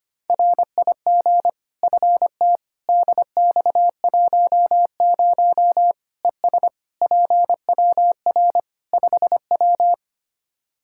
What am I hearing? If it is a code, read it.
Morse code, heard as a keyed tone, about 25 wpm, 700 Hz. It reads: RIG FT DX10 ES PWR 5W
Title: Practice QSO - Exchange 2 - Info - 25wpm